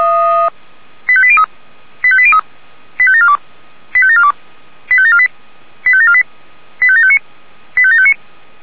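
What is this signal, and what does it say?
Five-tone selective call (ZVEI), heard as quick runs of five tones, tones of about 70 ms. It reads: 85803 85803 85742 85742 85748 85748 85759 85759
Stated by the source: ZVEI five-tone selective call received on SDR.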